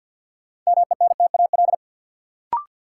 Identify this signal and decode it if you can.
Morse code, heard as a keyed tone, about 51 wpm, 700 Hz. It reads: MENTAL
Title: Sentences from Top 1000 Words - Review Part 4 - New Word Drill - 50wpm